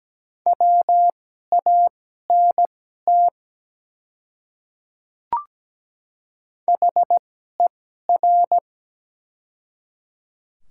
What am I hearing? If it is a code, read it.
Morse code, heard as a keyed tone, about 17 wpm, spaced as for 12 wpm, 700 Hz. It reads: WANT HER
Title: 2x Character Spacing - Top 200 Words - 17wpm